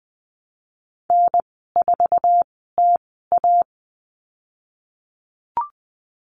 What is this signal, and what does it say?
Morse code, heard as a keyed tone, about 20 wpm, spaced as for 15 wpm, 700 Hz. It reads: N4TA
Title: 055 Lesson 10g - Letter r - Call Signs 20wpm